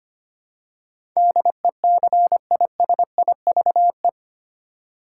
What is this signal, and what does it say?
Morse code, heard as a keyed tone, about 25 wpm, 700 Hz. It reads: DECISIVE
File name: Eight Letter Words - 25wpm